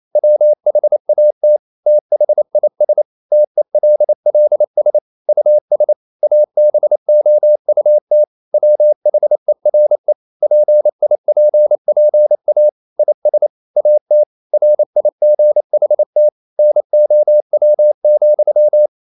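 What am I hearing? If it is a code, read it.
Morse code, heard as a keyed tone, about 28 wpm, 600 Hz. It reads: WHAT THIS TELLS US ABOUT WHERE PIPPA IS AT RIGHT NOW,
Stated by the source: Morse code practice at 28 wpm, text of Ken Follett's novel Never in English, from CW med Gnister.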